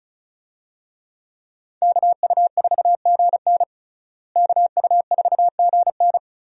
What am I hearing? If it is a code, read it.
Morse code, heard as a keyed tone, about 36 wpm, 700 Hz. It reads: KU4GN KU4GN
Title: USA Call Signs - 5 Characters - Repeated 3x - 35wpm